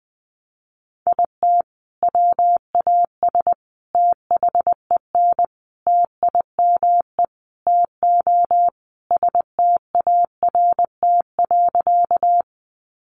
Morse code, heard as a keyed tone, about 20 wpm, 700 Hz. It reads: IT WAS THEN TIME TO START.
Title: Sentences from Top 300 Words 20wpm